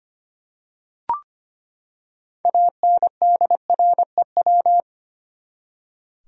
Morse code, heard as a keyed tone, about 25 wpm, 700 Hz. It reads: ANDREW
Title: USA Common Names - 25wpm